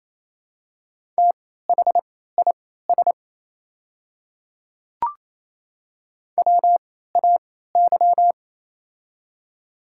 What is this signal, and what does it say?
Morse code, heard as a keyed tone, about 27 wpm, spaced as for 16 wpm, 700 Hz. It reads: THIS WAY
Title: 3x Character Spacing - Top 100 Words - 28wpm